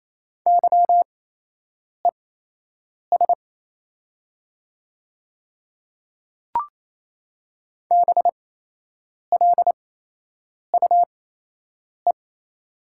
Morse code, heard as a keyed tone, about 28 wpm, spaced as for 8 wpm, 700 Hz. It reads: YES BLUE
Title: 8x Character Spacing - Top 500 Words - 28wpm